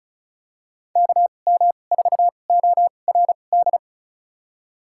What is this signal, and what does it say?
Morse code, heard as a keyed tone, about 36 wpm, spaced as for 25 wpm, 700 Hz. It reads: KM4ORD
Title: USA Call Signs - 6 Characters - 2x Character Spacing - 35wpm